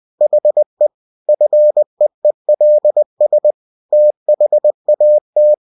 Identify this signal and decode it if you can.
Morse code, heard as a keyed tone, about 20 wpm, 600 Hz. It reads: HE FEELS THAT
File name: cw-books / winnie-the-pooh / winnie-20wpm-page-0000